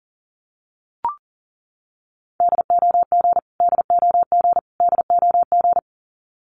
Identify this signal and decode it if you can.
Morse code, heard as a keyed tone, about 40 wpm, 700 Hz. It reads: DOG DOG DOG
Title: Top 500 Words Repeated 40wpm